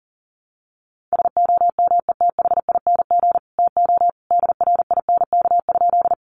Morse code, heard as a keyed tone, about 40 wpm, 700 Hz. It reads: SOMETHING TO DRINK?
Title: Sentences from Top 1500 Words 40wpm